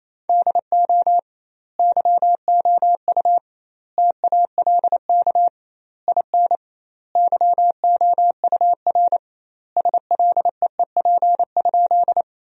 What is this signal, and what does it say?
Morse code, heard as a keyed tone, about 28 wpm, 700 Hz. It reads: DO YOU TALK IN YOUR SLEEP?
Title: 2x Word-Spacing - Sentences from Top 500 Words - 28wpm